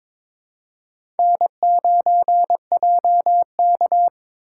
Morse code, heard as a keyed tone, about 22 wpm, 700 Hz. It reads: N9JK
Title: USA Call Signs - 4 Characters - 6x Character Spacing - 22wpm